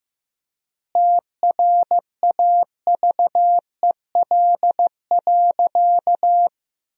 Morse code, heard as a keyed tone, about 15 wpm, 700 Hz. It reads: TRAVEL.
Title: 3x Word-Spacing - Sentences from Top 400 Words - 15wpm